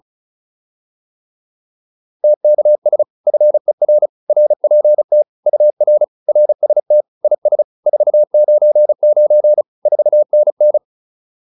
Morse code, heard as a keyed tone, about 35 wpm, 600 Hz. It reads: TKS FER RPT UR RST IS 499 4NN